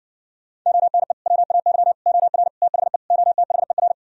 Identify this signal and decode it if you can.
Morse code, heard as a keyed tone, about 60 wpm, 700 Hz. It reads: ONE WAY OR THE OTHER